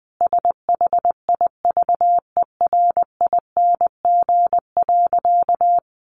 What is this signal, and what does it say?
Morse code, heard as a keyed tone, about 20 wpm, 700 Hz. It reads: SHIVERING.
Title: A Sound Of Thunder 20wpm